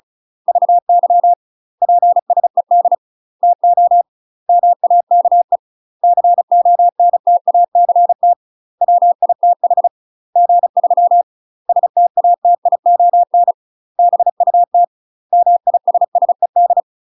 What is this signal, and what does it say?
Morse code, heard as a keyed tone, about 36 wpm, 700 Hz. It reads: VY PSED TO MAKE CONTACT WITH G3 STATION BUT MISSED
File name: UK MASTER QSOs TRAINING / 35WPM / UK MASTER QSOs 0020